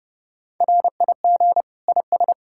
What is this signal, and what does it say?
Morse code, heard as a keyed tone, about 37 wpm, 700 Hz. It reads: RIG IS
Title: Speed-Racing Common QSO Phrases Encore - 30wpm